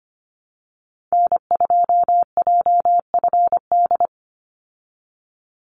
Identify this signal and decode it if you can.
Morse code, heard as a keyed tone, about 25 wpm, 700 Hz. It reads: N2JFD
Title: Call Signs Repeated 2x - 25wpm